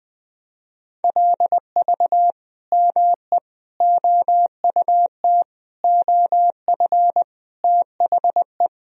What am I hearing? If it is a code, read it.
Morse code, heard as a keyed tone, about 20 wpm, 700 Hz. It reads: LV ME OUT OF THE
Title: Sentences from Top 400 Words and Abbreviations 20wpm